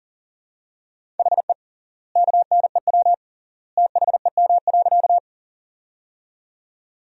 Morse code, heard as a keyed tone, about 40 wpm, 700 Hz. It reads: HE KNEW THEM.
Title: 3x Word-Spacing - Sentences from Top 400 Words - 40wpm